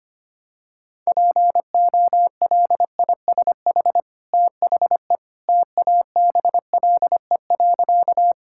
Morse code, heard as a keyed tone, about 25 wpm, 700 Hz. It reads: POLISH THE TABLE.
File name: Easy Sentences with Hints - 25wpm